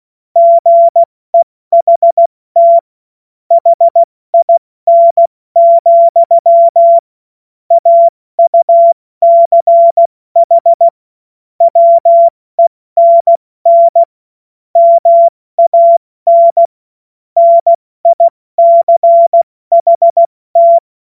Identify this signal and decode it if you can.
Morse code, heard as a keyed tone, about 16 wpm, 680 Hz. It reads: GEHT HIN, AUCH WENN MAN NICHT
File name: Morsekurs / Witze / Witz 08. Februar 2022 70 F80